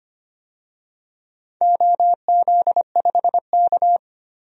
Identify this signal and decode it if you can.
Morse code, heard as a keyed tone, about 25 wpm, 700 Hz. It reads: OZ5K
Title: DXCC-100 Call Signs - 4 Characters - Repeated 2x - 25wpm